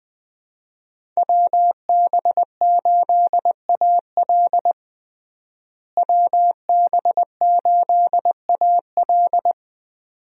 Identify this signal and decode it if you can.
Morse code, heard as a keyed tone, about 20 wpm, 700 Hz. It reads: WB8AL WB8AL
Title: USA Call Signs - 5 Characters - Repeated 2x - 20wpm